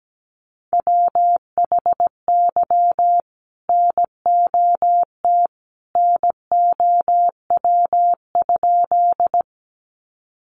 Morse code, heard as a keyed tone, about 17 wpm, 700 Hz. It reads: WHY NOT NOW?
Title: Sentences from Top 600 Words 17wpm